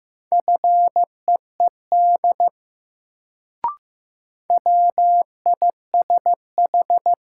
Morse code, heard as a keyed tone, about 15 wpm, 700 Hz. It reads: FEED WISH
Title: Sentences from Top 1600 Words - New Word Drill Alt - 15wpm